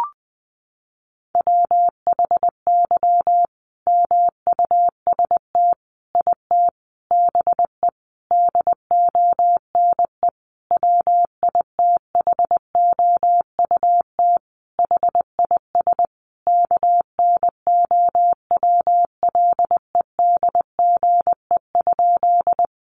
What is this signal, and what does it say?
Morse code, heard as a keyed tone, about 20 wpm, 700 Hz. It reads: WHY MUST IT BE DONE WITHOUT HIS KNOWLEDGE?
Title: Sentences from Top 600 Words 20wpm